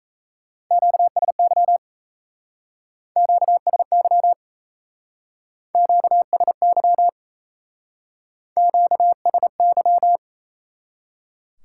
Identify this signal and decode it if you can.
Morse code, heard as a keyed tone, about 33 wpm, 700 Hz. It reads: QSY QSY QSY QSY
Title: Speed-Racing Top 180 QSO Elements 28wpm